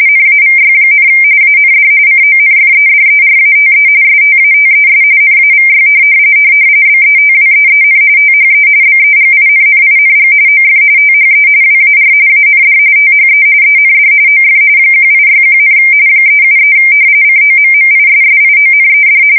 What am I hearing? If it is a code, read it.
RTTY, about 45.45 baud. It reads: NAVY, AND CONFEDERATE STATES NAVY BETWEEN THE BEGINNING OF THE WAR AND THE END OF 1861, 373 COMMISSIONED
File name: fsk170 / usnavy3